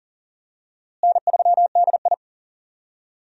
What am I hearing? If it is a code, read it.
Morse code, heard as a keyed tone, about 40 wpm, 700 Hz. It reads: N3DI
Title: USA Call Signs - 4 Characters - Repeated 3x - 40wpm